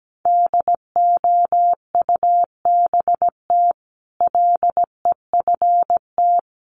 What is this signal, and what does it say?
Morse code, heard as a keyed tone, about 17 wpm, 700 Hz. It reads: DOUBT LEFT
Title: Sentences from Top 1500 Words 17wpm